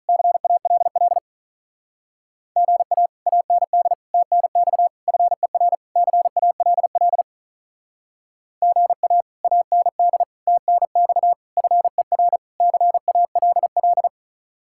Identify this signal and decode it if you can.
Morse code, heard as a keyed tone, about 44 wpm, 700 Hz. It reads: CALL GA AND TNX FER CALL GA AND TNX FER CALL
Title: Speed-Racing Common QSO Phrases Encore - 35wpm